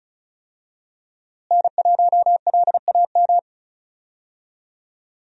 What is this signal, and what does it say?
Morse code, heard as a keyed tone, about 36 wpm, 700 Hz. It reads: N1LAM
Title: USA Call Signs - 5 Characters - Repeated 3x - 35wpm